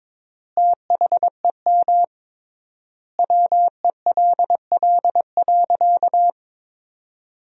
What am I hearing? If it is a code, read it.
Morse code, heard as a keyed tone, about 22 wpm, 700 Hz. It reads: THEM WELL.
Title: 3x Word-Spacing - Sentences from Top 300 Words - 22wpm